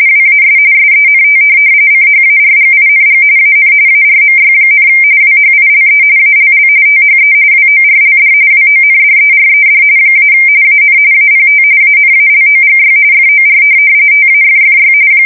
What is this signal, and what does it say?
RTTY, about 45.45 baud. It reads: WAS SENT CHARGING AN ARMY, WHILE ALL THE WORLD WONDERED - AN APT ANALOGY FOR TAFFY
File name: fsk170 / USS-New-Jersey